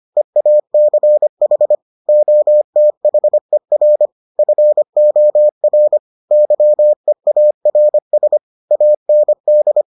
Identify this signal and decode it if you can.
Morse code, heard as a keyed tone, about 25 wpm, 600 Hz. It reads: EACH OTHER FOR YEARS AND